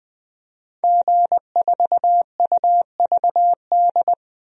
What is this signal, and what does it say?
Morse code, heard as a keyed tone, about 20 wpm, 700 Hz. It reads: G4UVD
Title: DXCC-100 Call Signs - 5 Characters - Repeated 2x - 20wpm